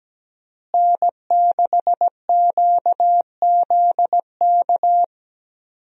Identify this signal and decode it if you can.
Morse code, heard as a keyed tone, about 17 wpm, 700 Hz. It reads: N6QZK
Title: USA Call Signs - 5 Characters - Repeated 3x - 17wpm